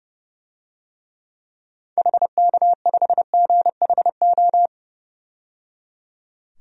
Morse code, heard as a keyed tone, about 30 wpm, 700 Hz. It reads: HK5GHO
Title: DXCC-100 Call Signs - 6 Characters - 30wpm